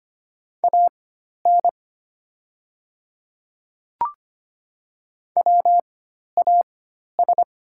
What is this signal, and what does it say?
Morse code, heard as a keyed tone, about 25 wpm, spaced as for 12 wpm, 700 Hz. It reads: AN WAS
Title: 4x Character Spacing - Top 100 Words - 25wpm